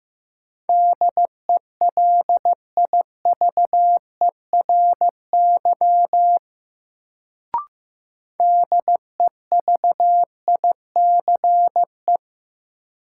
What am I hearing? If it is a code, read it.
Morse code, heard as a keyed tone, about 15 wpm, 700 Hz. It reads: DELIVERY DEVICE
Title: Sentences from Top 1500 Words - New Word Drill Alt - 15wpm